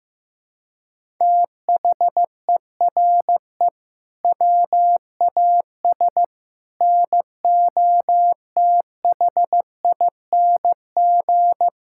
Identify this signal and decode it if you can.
Morse code, heard as a keyed tone, about 15 wpm, 700 Hz. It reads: THERE WAS NOTHING
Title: Sentences from Top 1000 Words - Review Part 5 - 15wpm